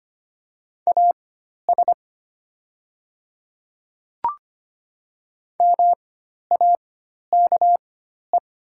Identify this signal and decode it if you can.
Morse code, heard as a keyed tone, about 25 wpm, spaced as for 12 wpm, 700 Hz. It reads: AS MAKE
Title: 4x Character Spacing - Top 100 Words - 25wpm